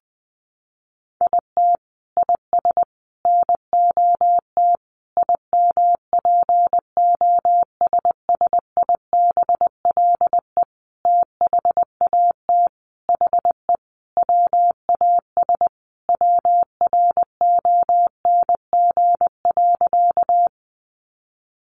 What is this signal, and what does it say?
Morse code, heard as a keyed tone, about 20 wpm, 700 Hz. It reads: IT IS NOT IMPOSSIBLE THAT HE WAS WRONG.
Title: Sentences from Top 1100 Words 20wpm